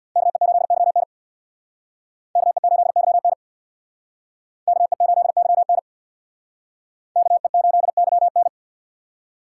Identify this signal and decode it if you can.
Morse code, heard as a keyed tone, about 62 wpm, 700 Hz. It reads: KE8YN KE8YN KE8YN KE8YN
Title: Speed Racing - USA 2x2 Call Signs 50wpm